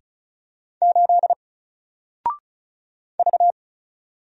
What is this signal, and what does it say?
Morse code, heard as a keyed tone, about 36 wpm, 700 Hz. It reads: 8 V
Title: Single Letter-Number 35wpm